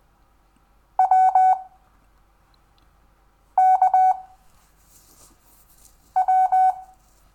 Morse code, heard as a keyed tone, about 20 wpm, 750 Hz. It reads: W K W
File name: cw / Lesson6 WI